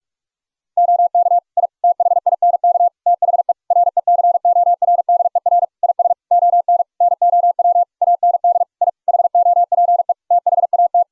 Morse code, heard as a keyed tone, about 44 wpm, 700 Hz. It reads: OK I THINK THE RECORDER IS ON NOW AND I HOPE THAT